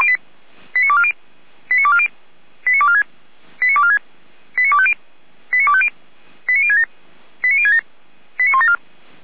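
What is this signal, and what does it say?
Five-tone selective call (ZVEI), heard as quick runs of five tones, tones of about 70 ms. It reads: 89240 89240 89246 89246 89250 89250 89067 89067 89174
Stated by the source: Five-tone ZVEI selective call received on SDR, from the audio archive of Feuerwehr Orlamünde.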